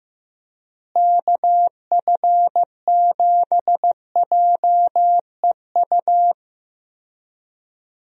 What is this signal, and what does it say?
Morse code, heard as a keyed tone, about 16 wpm, 700 Hz. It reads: KF7JEU
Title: USA Call Signs - 6 Characters - Repeated 3x - 15wpm